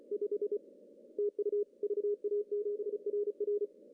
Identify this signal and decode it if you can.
Morse code, heard as a keyed tone, about 30 wpm, 410 Hz. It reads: 5 TU VA7RR
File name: CW / cqww-cw-k5tr